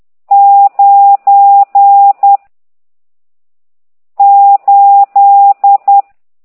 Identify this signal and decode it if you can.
Morse code, heard as a keyed tone, about 10 wpm, 800 Hz. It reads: 9 8
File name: ct2idw / exerc14